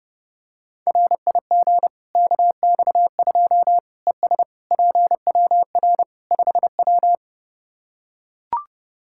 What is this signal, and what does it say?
Morse code, heard as a keyed tone, about 32 wpm, 700 Hz. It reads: RIG KX2 ES PWR 5W
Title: K1USN SST QSO Practice 30wpm